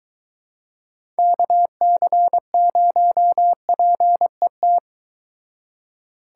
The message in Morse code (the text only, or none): KC0PET